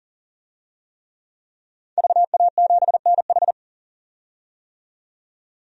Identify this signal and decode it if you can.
Morse code, heard as a keyed tone, about 40 wpm, 700 Hz. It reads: VA7NH
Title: DXCC-100 Call Signs - 5 Characters - 40wpm